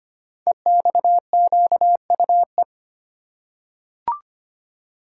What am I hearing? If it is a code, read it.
Morse code, heard as a keyed tone, about 25 wpm, 700 Hz. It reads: EXQUE
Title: Code Groups - 5 Letter:Numbers - 25wpm